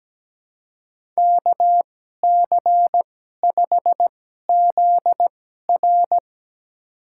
Morse code, heard as a keyed tone, about 17 wpm, spaced as for 12 wpm, 700 Hz. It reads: KC5ZR
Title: USA Call Signs - 5 Characters - 2x Character Spacing - 17wpm